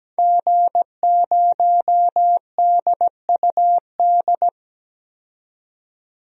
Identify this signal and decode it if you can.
Morse code, heard as a keyed tone, about 17 wpm, 700 Hz. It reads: G0DUD
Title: DXCC-100 Call Signs - 5 Characters - 17wpm